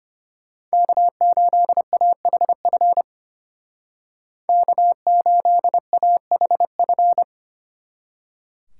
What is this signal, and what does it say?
Morse code, heard as a keyed tone, about 31 wpm, 700 Hz. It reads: K8AHF K8AHF